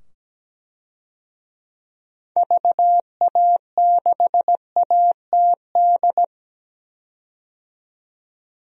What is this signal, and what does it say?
Morse code, heard as a keyed tone, about 17 wpm, 700 Hz. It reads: VA6ATD
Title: DXCC-100 Call Signs - 6 Characters - 17wpm